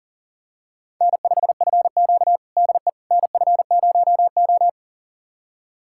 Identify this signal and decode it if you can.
Morse code, heard as a keyed tone, about 40 wpm, 700 Hz. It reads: N5FQ DE NF0O
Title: USA Callsign DE Callsign - 4 Characters - 40wpm